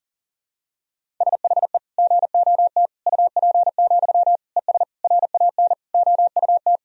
Morse code, heard as a keyed tone, about 40 wpm, 700 Hz. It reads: SHE GOT UP, ES RAN OUT
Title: Sentences from Top 500 Words and Abbreviations 40wpm